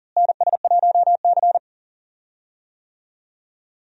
Morse code, heard as a keyed tone, about 40 wpm, 700 Hz. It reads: NS1C
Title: USA Call Signs - 4 Characters - Repeated 3x - 40wpm